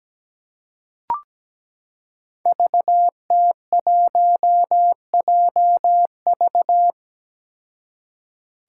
Morse code, heard as a keyed tone, about 17 wpm, 700 Hz. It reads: VT1JV
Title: Five Challenging Letter-Number Combinations 17wpm